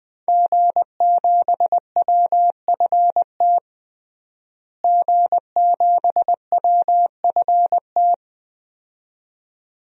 Morse code, heard as a keyed tone, about 20 wpm, 700 Hz. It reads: G7WFT G7WFT
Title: DXCC-100 Call Signs - 5 Characters - Repeated 2x - 20wpm